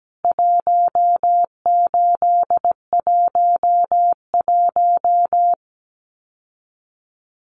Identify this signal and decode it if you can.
Morse code, heard as a keyed tone, about 17 wpm, 700 Hz. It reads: 1811
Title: CWT Exchange 17wpm